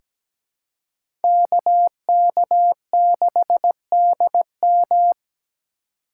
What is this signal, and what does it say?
Morse code, heard as a keyed tone, about 17 wpm, 700 Hz. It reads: KK6DM
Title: USA Call Signs - 4x Character Spacing - 17wpm